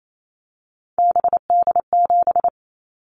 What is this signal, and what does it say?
Morse code, heard as a keyed tone, about 28 wpm, 700 Hz. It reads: BD7
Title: Callsign Prefixes for Top 100 DXCC - 28wpm